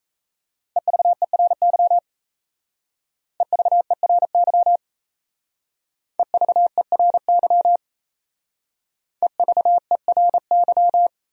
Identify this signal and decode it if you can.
Morse code, heard as a keyed tone, about 34 wpm, 700 Hz. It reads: EVERY EVERY EVERY EVERY
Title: Speed-Racing Top 200 Words 28wpm